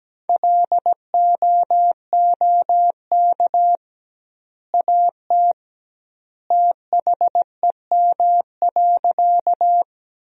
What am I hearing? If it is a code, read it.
Morse code, heard as a keyed tone, about 17 wpm, 700 Hz. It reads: LOOK AT THEM.
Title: 2x Word-Spacing - Sentences from Top 100 Words - 17wpm